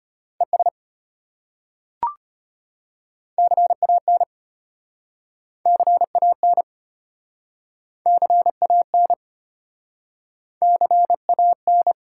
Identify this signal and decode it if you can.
Morse code, heard as a keyed tone, about 32 wpm, 700 Hz. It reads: ES CAN CAN CAN CAN